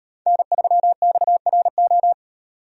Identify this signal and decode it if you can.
Morse code, heard as a keyed tone, about 38 wpm, 700 Hz. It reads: N3XRO